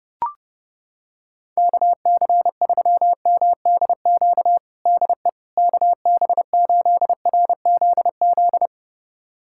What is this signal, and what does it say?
Morse code, heard as a keyed tone, about 30 wpm, 700 Hz. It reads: KC3MDQ DE KB8RZZ
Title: USA Callsign DE Callsign 30wpm